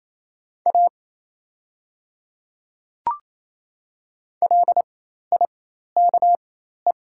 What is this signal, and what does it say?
Morse code, heard as a keyed tone, about 28 wpm, spaced as for 13 wpm, 690 Hz. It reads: A LIKE
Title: 4x Character Spacing - Top 100 Words - 28wpm